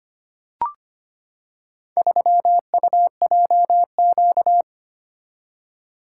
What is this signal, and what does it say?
Morse code, heard as a keyed tone, about 25 wpm, 700 Hz. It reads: 3UJQ
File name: practice / 25wpm / Four Letter-Number Combinations - 25wpm